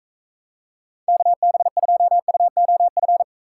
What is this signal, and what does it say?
Morse code, heard as a keyed tone, about 40 wpm, 700 Hz. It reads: KB2UOF